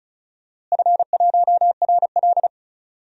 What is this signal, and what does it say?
Morse code, heard as a keyed tone, about 34 wpm, 700 Hz. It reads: F1RL
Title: DXCC-100 Call Signs - 4 Characters - Repeated 2x - 35wpm